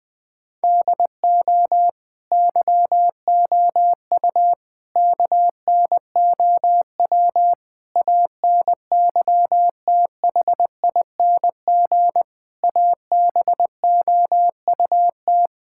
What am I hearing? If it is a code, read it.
Morse code, heard as a keyed tone, about 20 wpm, 700 Hz. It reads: DO YOU KNOW ANYTHING ABOUT
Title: Sentences from Top 1600 Words 20wpm